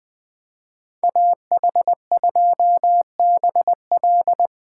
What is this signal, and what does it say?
Morse code, heard as a keyed tone, about 20 wpm, 700 Hz. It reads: AH2BL
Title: Speed Racing - USA Call Signs 17wpm